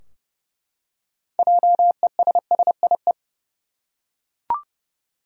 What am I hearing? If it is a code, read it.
Morse code, heard as a keyed tone, about 31 wpm, 700 Hz. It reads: JESSIE
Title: USA Common Names - 30wpm